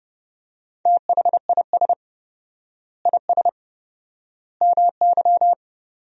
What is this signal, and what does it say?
Morse code, heard as a keyed tone, about 30 wpm, 700 Hz. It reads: THIS IS MY
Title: 4x Word-Spacing - Sentences from Top 400 Words - 30wpm